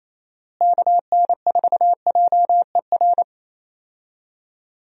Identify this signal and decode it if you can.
Morse code, heard as a keyed tone, about 27 wpm, 700 Hz. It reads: KN4JER
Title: USA Call Signs - 6 Characters - Repeated 2x - 28wpm